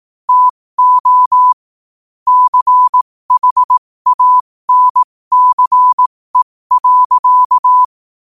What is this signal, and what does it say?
Morse code, heard as a keyed tone, about 18 wpm, spaced as for 15 wpm, 1000 Hz. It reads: TO CHANCE.
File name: qotdfiles / 15wpm